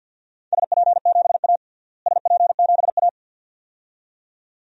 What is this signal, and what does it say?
Morse code, heard as a keyed tone, about 52 wpm, 700 Hz. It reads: SP7A SP7A
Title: DXCC-100 Call Signs - 4 Characters - Repeated 2x - 50wpm